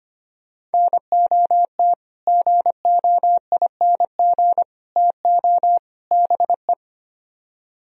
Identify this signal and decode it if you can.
Morse code, heard as a keyed tone, about 33 wpm, 700 Hz. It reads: NOT GOING TO BE